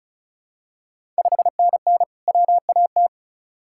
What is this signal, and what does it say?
Morse code, heard as a keyed tone, about 36 wpm, 700 Hz. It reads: 5NN WAT
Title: Ontario QSO Party Exchange 35wpm